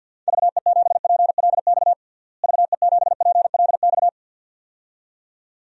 Morse code, heard as a keyed tone, about 52 wpm, 700 Hz. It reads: VE7PLX VE7PLX
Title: DXCC-100 Call Signs - 6 Characters - Repeated 2x - 50wpm